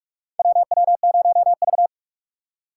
Morse code, heard as a keyed tone, about 46 wpm, 700 Hz. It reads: WW0V
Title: USA Call Signs - 4 Characters - 2x Character Spacing - 45wpm